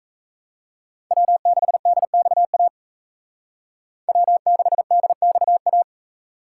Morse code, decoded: W6DXA W6DXA

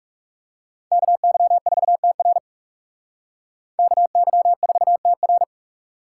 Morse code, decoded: KY4TR KY4TR

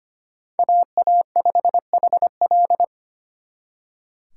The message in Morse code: AA5HL

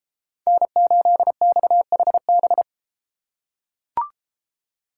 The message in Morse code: N8XHB

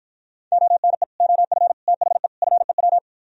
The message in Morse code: ONE OF THE FEW

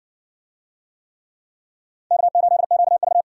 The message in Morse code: K8QV

